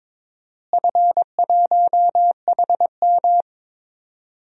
Morse code, heard as a keyed tone, about 22 wpm, 700 Hz. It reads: F1HM
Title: DXCC-100 Call Signs - 4 Characters - Repeated 2x - 22wpm